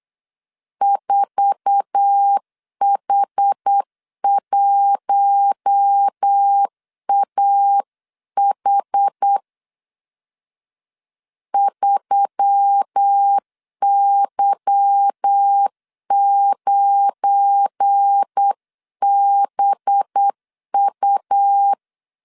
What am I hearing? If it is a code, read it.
Morse code, heard as a keyed tone, about 8 wpm, 800 Hz. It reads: 4H1AH 3Y9BU
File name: CW / DET / 1 9 1500C S C 8wpm 5653